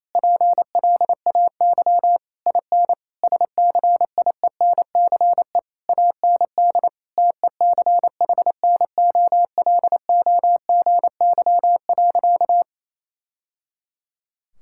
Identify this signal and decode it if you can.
Morse code, heard as a keyed tone, about 27 wpm, 700 Hz. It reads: PLAY IN SCIENCE AND TECHNOLOGY.
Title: Sentences from Top 1000 Words - Review Part 6 - 28wpm